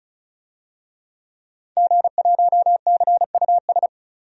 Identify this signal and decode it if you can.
Morse code, heard as a keyed tone, about 34 wpm, 700 Hz. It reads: G1CUS
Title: DXCC-100 Call Signs - 5 Characters - 35wpm